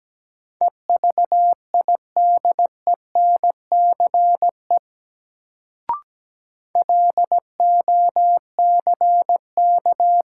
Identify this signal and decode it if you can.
Morse code, heard as a keyed tone, about 17 wpm, 700 Hz. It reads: EVIDENCE LOCK
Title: Sentences from Top 1300 Words - New Word Drill Alt - 17wpm